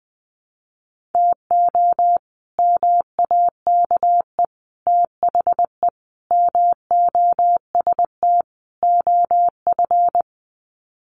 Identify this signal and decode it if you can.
Morse code, heard as a keyed tone, about 20 wpm, 700 Hz. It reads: TO MAKE THE MOST OF
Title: Sets of 5 Words Encore2 20wpm